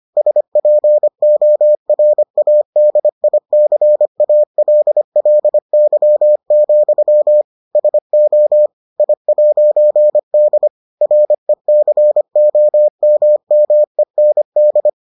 Morse code, heard as a keyed tone, about 25 wpm, 600 Hz. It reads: SPORADICALLY, SO I'D RECOMMEND